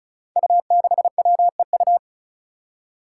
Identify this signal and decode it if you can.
Morse code, heard as a keyed tone, about 36 wpm, 700 Hz. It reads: U6WEU
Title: Five Challenging Letter-Number Combinations 35wpm